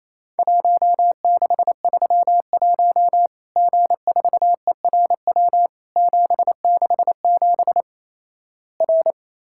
Morse code, heard as a keyed tone, about 28 wpm, 700 Hz. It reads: 1631 G4ERW 767 R
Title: Worked All Europe DX Contest - QTC Exchange - 28wpm